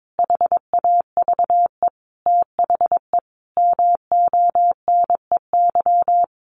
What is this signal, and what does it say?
Morse code, heard as a keyed tone, about 22 wpm, 700 Hz. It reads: HAVE THE MONEY
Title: Sentences from Top 1500 Words 22wpm